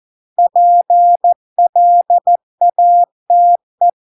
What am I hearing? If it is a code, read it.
Morse code, heard as a keyed tone, about 14 wpm, 700 Hz. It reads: PLATE